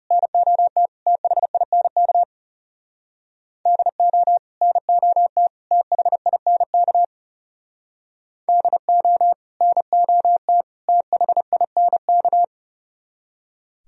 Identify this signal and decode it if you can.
Morse code, heard as a keyed tone, about 37 wpm, 690 Hz. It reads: NOT THINK DO NOT THINK DO NOT THINK
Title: Speed-Racing - Sets of 3 Words 30wpm